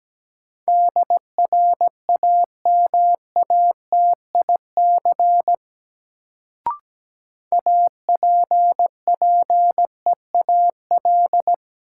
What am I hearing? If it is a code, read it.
Morse code, heard as a keyed tone, about 17 wpm, 700 Hz. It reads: DRAMATIC APPEAL
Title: Sentences from Top 1300 Words - New Word Drill - 17wpm